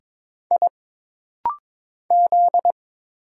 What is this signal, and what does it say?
Morse code, heard as a keyed tone, about 22 wpm, 700 Hz. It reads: I Z
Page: This rapid-fire timing is geared to push your ICR (Instant Character Recognition) skills.